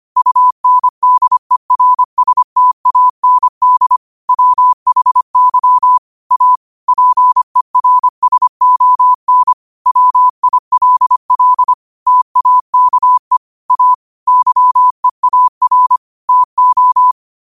Morse code, heard as a keyed tone, about 25 wpm, 1000 Hz. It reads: UNDERSTAND WHY A PERSON WILL TAKE A YEAR TO
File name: qotdfiles / 25wpm